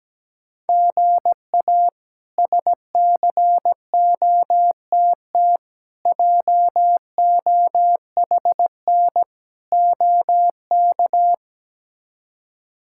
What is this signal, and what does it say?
Morse code, heard as a keyed tone, about 17 wpm, 700 Hz. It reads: GA SCOTT JOHN OK
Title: K1USN SST QSO Practice - 17wpm